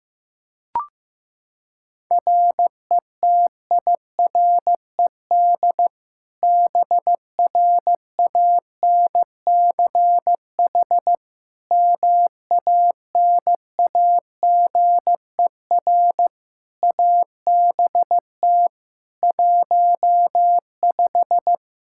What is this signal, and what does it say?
Morse code, heard as a keyed tone, about 15 wpm, 700 Hz. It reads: RETIRED BRANCH MANAGER ABT 15
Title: Practice QSO - Exchange 3 - More Info - 15wpm